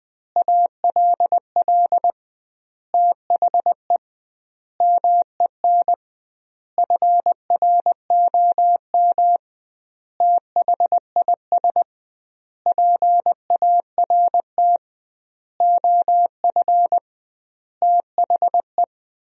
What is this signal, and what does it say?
Morse code, heard as a keyed tone, about 20 wpm, 700 Hz. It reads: ALL THE MEN FROM THIS PART OF THE
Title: 2x Word-Spacing - Sentences from Top 300 Words - 20wpm